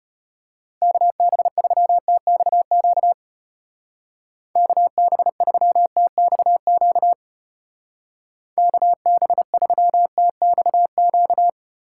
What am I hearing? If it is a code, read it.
Morse code, heard as a keyed tone, about 32 wpm, 700 Hz. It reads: KB3TXQ KB3TXQ KB3TXQ